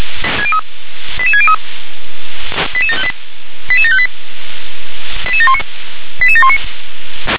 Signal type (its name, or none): five-tone selective call